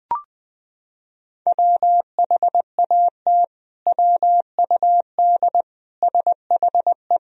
Morse code, heard as a keyed tone, about 20 wpm, 700 Hz. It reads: WHAT WUD SHE
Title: Sentences from Top 100 Words and Abbreviations 20wpm